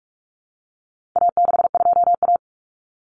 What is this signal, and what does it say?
Morse code, heard as a keyed tone, about 44 wpm, 700 Hz. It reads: A62A
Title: Common Cut Numbers - 4-Digit 45wpm